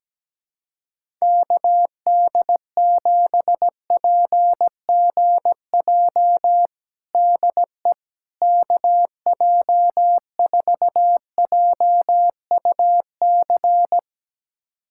Morse code, heard as a keyed tone, about 17 wpm, 700 Hz. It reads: KD7PGJ DE KJ4JUC
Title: USA Callsign DE Callsign 17wpm